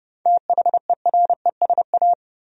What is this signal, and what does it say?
Morse code, heard as a keyed tone, about 30 wpm, 700 Hz. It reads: THERESA
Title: USA Common Names - 30wpm